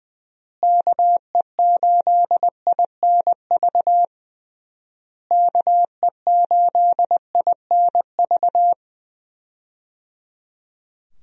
Morse code, heard as a keyed tone, about 20 wpm, 700 Hz. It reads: KE8INV KE8INV
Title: USA Call Signs - 6 Characters - Repeated 3x - 20wpm